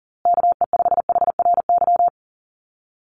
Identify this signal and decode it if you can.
Morse code, heard as a keyed tone, about 40 wpm, 700 Hz. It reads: KE5HRY
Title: USA Call Signs Repeated 2x 40wpm